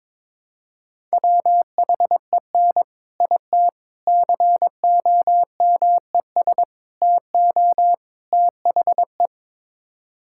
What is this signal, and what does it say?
Morse code, heard as a keyed tone, about 22 wpm, 700 Hz. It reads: WHEN IT COMES TO THE
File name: practice / 22wpm / Build A Phrase - 5-Word Phrases - 22wpm